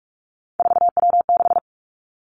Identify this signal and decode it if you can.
Morse code, heard as a keyed tone, about 46 wpm, 700 Hz. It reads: 4W6
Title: Three Challenging Letter-Number Combinations 45wpm